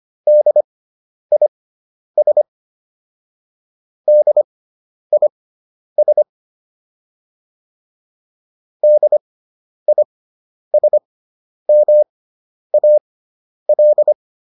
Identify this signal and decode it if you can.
Morse code, heard as a keyed tone, about 25 wpm, spaced as for 10 wpm, 600 Hz. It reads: DIS DIS DISMAL